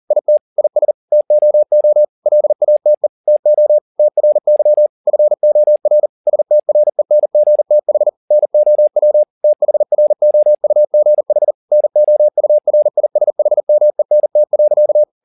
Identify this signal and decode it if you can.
Morse code, heard as a keyed tone, about 40 wpm, 600 Hz. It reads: IT IS TOO LATE TO TRY FOR STRENGTH NOW THROUGH NOURISHMENT.
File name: the old man and the sea 2 40wpm